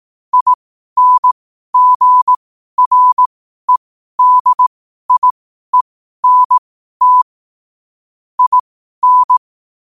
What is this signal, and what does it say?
Morse code, heard as a keyed tone, about 18 wpm, spaced as for 12 wpm, 1000 Hz. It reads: INGREDIENT IN